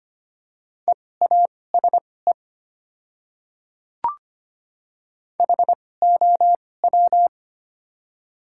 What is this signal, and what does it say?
Morse code, heard as a keyed tone, about 25 wpm, spaced as for 18 wpm, 700 Hz. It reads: EASE HOW
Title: 2x Character Spacing - Top 300 Words - 25wpm